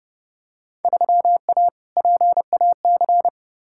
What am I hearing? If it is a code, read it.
Morse code, heard as a keyed tone, about 31 wpm, 700 Hz. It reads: 3A PAC
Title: Field Day Exchanges Encore - 30wpm